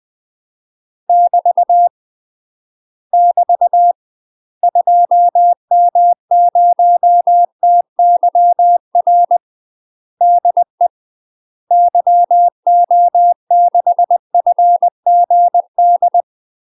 Morse code, heard as a keyed tone, about 20 wpm, 700 Hz. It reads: <BT> <BT> 2M0TYR DE YO6FGD
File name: UK MASTER QSOs TRAINING / 20WPM / UK MASTER QSOs 0010